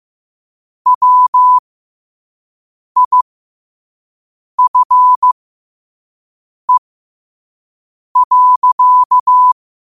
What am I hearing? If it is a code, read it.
Morse code, heard as a keyed tone, about 15 wpm, spaced as for 5 wpm, 1000 Hz. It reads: WIFE.
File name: qotdfiles / 5wpm